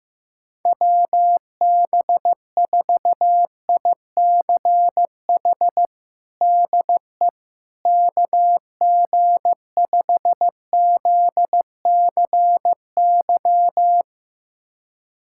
Morse code, heard as a keyed tone, about 15 wpm, 700 Hz. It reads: WB4ICH DE KG5ZCY
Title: USA Callsign DE Callsign - 6 Characters - 15wpm